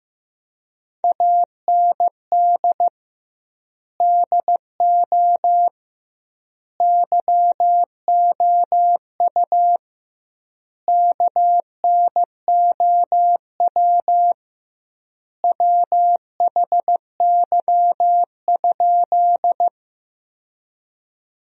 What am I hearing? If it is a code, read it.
Morse code, heard as a keyed tone, about 15 wpm, 700 Hz. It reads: AND DO YOU KNOW WHY?
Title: 2x Word-Spacing - Sentences from Top 200 Words - 15wpm